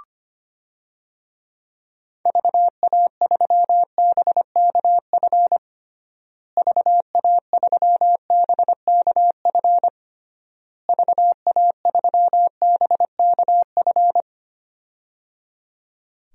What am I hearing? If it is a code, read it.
Morse code, heard as a keyed tone, about 25 wpm, 700 Hz. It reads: VA3BKF VA3BKF VA3BKF
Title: DXCC-100 Call Signs - 6 Characters - Repeated 3x - 25wpm